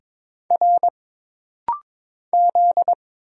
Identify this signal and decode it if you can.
Morse code, heard as a keyed tone, about 22 wpm, 700 Hz. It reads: R Z
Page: This rapid-fire timing is geared to push your ICR (Instant Character Recognition) skills.